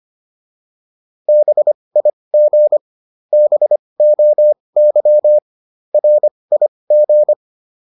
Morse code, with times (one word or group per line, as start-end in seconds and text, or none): 1.3-2.8 BIG
3.3-5.4 BOY
5.9-7.3 RIG